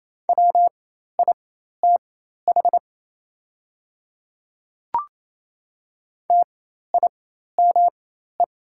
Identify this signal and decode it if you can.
Morse code, heard as a keyed tone, about 28 wpm, spaced as for 13 wpm, 700 Hz. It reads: WITH TIME
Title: 4x Character Spacing - Top 200 Words - 28wpm